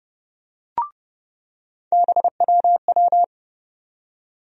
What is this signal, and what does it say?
Morse code, heard as a keyed tone, about 32 wpm, 700 Hz. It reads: BWW
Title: USA Call Sign Postfixes - 3 Letters 30wpm